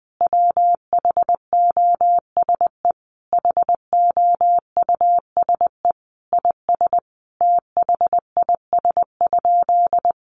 Morse code, heard as a keyed tone, about 20 wpm, 700 Hz. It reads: WHOSE HOUSE IS THIS?
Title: Sentences from Top 1200 Words 20wpm